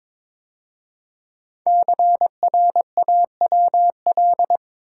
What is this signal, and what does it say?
Morse code, heard as a keyed tone, about 22 wpm, 700 Hz. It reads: CRAWL
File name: 5 Letter Words - First 3 Letters Given - 22wpm